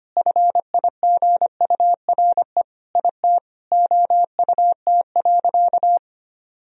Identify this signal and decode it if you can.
Morse code, heard as a keyed tone, about 25 wpm, 700 Hz. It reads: FIGURE IT OUT.
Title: Sentences from Top 1000 Words - Review Part 2 - 25wpm